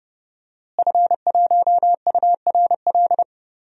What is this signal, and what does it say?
Morse code, heard as a keyed tone, about 32 wpm, 700 Hz. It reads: F1URL
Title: DXCC-100 Call Signs - 5 Characters - Repeated 3x - 30wpm